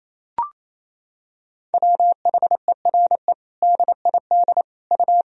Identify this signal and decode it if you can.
Morse code, heard as a keyed tone, about 28 wpm, 700 Hz. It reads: WHERE DID U
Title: Sentences from Top 200 Words and Abbreviations 28wpm